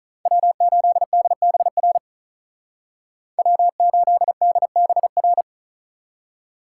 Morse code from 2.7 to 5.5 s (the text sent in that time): W8DBR